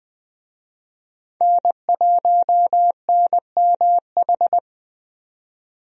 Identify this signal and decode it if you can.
Morse code, heard as a keyed tone, about 20 wpm, 700 Hz. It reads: N1NMH
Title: USA Call Signs - 5 Characters - 20wpm